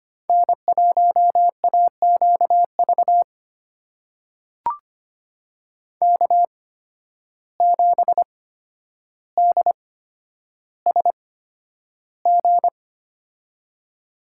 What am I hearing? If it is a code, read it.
Morse code, heard as a keyed tone, about 25 wpm, spaced as for 11 wpm, 700 Hz. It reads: N1AQV K7DSG
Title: USA Call Signs - 5 Characters - 8x Character Spacing - 25wpm